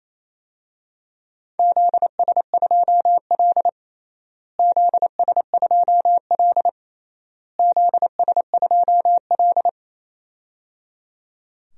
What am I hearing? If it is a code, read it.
Morse code, heard as a keyed tone, about 28 wpm, 700 Hz. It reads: ZS2L ZS2L ZS2L
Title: DXCC-100 Call Signs - 4 Characters - Repeated 3x - 28wpm